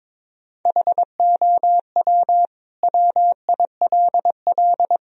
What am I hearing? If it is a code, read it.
Morse code, heard as a keyed tone, about 22 wpm, 700 Hz. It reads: HOW WILL
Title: Sentences from Top 1000 Words - Review Part 2 - 22wpm